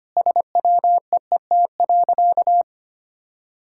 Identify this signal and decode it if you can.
Morse code, heard as a keyed tone, about 25 wpm, 700 Hz. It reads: SWEET.
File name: practice / 25wpm / Sentences with 3-Word Phrases - 25wpm